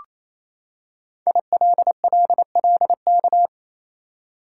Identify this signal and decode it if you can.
Morse code, heard as a keyed tone, about 27 wpm, 700 Hz. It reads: ILLLK
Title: Code Groups - 5 Letters - 28wpm